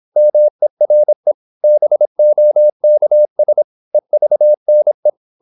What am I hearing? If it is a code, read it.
Morse code, heard as a keyed tone, about 26 wpm, 600 Hz. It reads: MERE BOKS EVNE
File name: Korte ord DK 26wpm